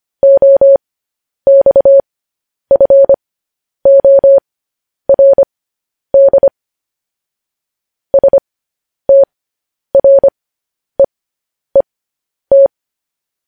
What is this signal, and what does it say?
Morse code, heard as a keyed tone, about 25 wpm, spaced as for 10 wpm, 550 Hz. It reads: OXFORD STREET
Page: news-headlines-in-morse-code-at-10-wpm